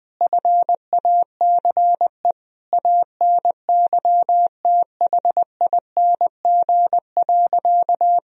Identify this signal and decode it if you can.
Morse code, heard as a keyed tone, about 20 wpm, 700 Hz. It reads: FACE ANYTHING.
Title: Sentences from Top 1000 Words - Review Part 2 - 20wpm